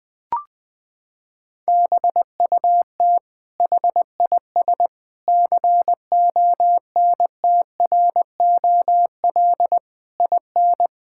Morse code, heard as a keyed tone, about 20 wpm, 700 Hz. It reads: BUT HIS CONTROL IN
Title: Sentences from Top 1000 Words - Review Part 6 - 20wpm